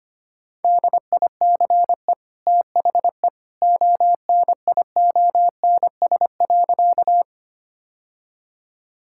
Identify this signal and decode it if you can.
Morse code, heard as a keyed tone, about 25 wpm, 700 Hz. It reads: DICE THE ONIONS.
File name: Easy Sentences with Hints - 25wpm